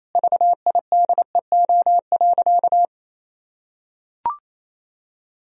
Morse code, heard as a keyed tone, about 28 wpm, 700 Hz. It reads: VIDEO.
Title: Sentences from Top 1000 Words - Review Part 4 - 28wpm